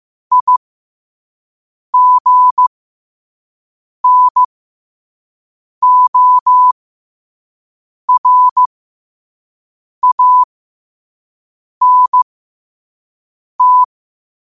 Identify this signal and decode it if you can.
Morse code, heard as a keyed tone, about 15 wpm, spaced as for 5 wpm, 1000 Hz. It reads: IGNORANT
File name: qotdfiles / 5wpm